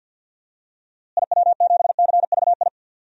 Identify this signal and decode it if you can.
Morse code, heard as a keyed tone, about 46 wpm, 700 Hz. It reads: IW7CVI
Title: DXCC-100 Call Signs - 6 Characters - Repeated 3x - 50wpm